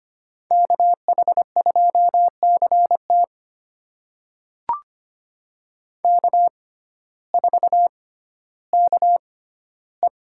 Morse code, heard as a keyed tone, about 25 wpm, spaced as for 14 wpm, 700 Hz. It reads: KH2CT K4KE
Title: USA Call Signs - 5 Characters - 6x Character Spacing - 25wpm